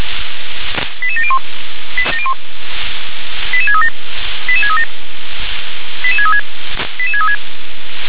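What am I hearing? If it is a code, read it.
Five-tone selective call (ZVEI), heard as quick runs of five tones, tones of about 70 ms. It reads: 88701 88637 88637 88636 88636
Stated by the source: Received on an SDR.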